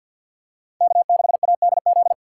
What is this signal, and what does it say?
Morse code, heard as a keyed tone, about 52 wpm, 700 Hz. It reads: K6ADZ
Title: USA Call Signs - 2x Character Spacing - 50wpm